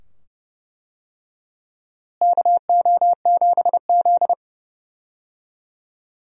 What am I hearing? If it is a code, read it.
Morse code, heard as a keyed tone, about 31 wpm, 700 Hz. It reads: KO7Z
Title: USA Call Signs - 4 Characters - Repeated 2x - 30wpm